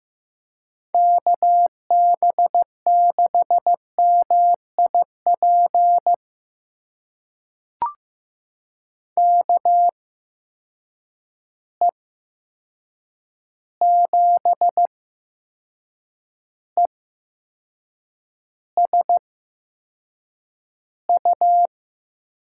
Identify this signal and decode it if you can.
Morse code, heard as a keyed tone, about 16 wpm, spaced as for 6 wpm, 700 Hz. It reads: KB6MIP KE7ESU
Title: USA Call Signs - 6 Characters - 8x Character Spacing - 15wpm